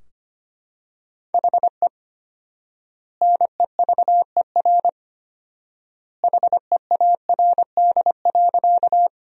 Morse code, heard as a keyed tone, about 25 wpm, 700 Hz. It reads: HE NEVER HEARD.
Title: 4x Word-Spacing - Sentences from Top 400 Words - 25wpm